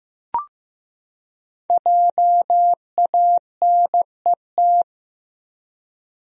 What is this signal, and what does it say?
Morse code, heard as a keyed tone, about 15 wpm, 700 Hz. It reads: JANET
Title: USA Common Names - 15wpm